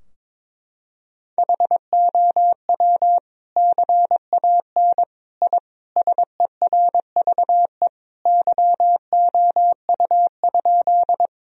Morse code, heard as a keyed tone, about 22 wpm, 700 Hz. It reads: HOW CAN I SERVE YOU?
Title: Sentences from Top 1000 Words - Review Part 2 - 22wpm